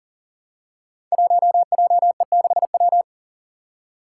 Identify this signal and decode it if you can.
Morse code, heard as a keyed tone, about 38 wpm, 700 Hz. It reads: 1JE6W
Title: Five Challenging Letter-Number Combinations 40wpm